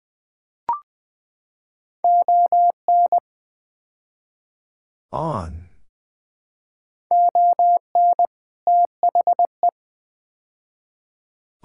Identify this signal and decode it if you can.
Morse code, heard as a keyed tone, about 20 wpm, 700 Hz. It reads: ON ON THE
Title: Build A Phrase - 5-Word Phrases - 20wpm